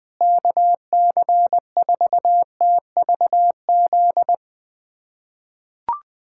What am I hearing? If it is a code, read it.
Morse code, heard as a keyed tone, about 20 wpm, 690 Hz. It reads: KC4TVZ
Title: POTA QSO Practice - 20wpm